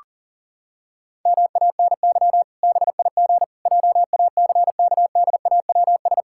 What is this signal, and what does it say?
Morse code, heard as a keyed tone, about 40 wpm, 700 Hz. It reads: MANY BIG JACKDAWS
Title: Pangram Challenge v2 - 40wpm